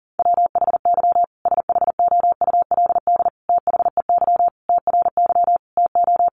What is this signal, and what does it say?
Morse code, heard as a keyed tone, about 40 wpm, 700 Hz. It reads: WHY SHOULD THEY TRY TO
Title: Sentences from Top 1100 Words 40wpm